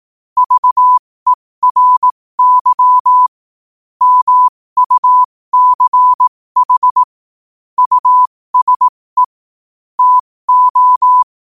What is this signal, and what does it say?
Morse code, heard as a keyed tone, about 18 wpm, spaced as for 15 wpm, 1000 Hz. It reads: VERY MUCH USE TO